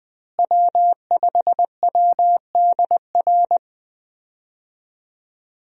Morse code, heard as a keyed tone, about 20 wpm, 700 Hz. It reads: W5WDR
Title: USA Call Signs - 5 Characters - Repeated 2x - 20wpm